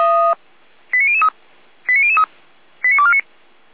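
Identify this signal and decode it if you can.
Five-tone selective call (ZVEI), heard as quick runs of five tones, tones of about 70 ms. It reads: 89003 89003 89249
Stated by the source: ZVEI five-tone call received with an SDR.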